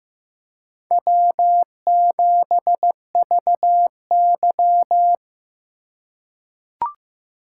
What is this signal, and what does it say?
Morse code, heard as a keyed tone, about 15 wpm, 700 Hz. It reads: W7VY
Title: USA Call Signs - 4 Characters - 8x Character Spacing - 15wpm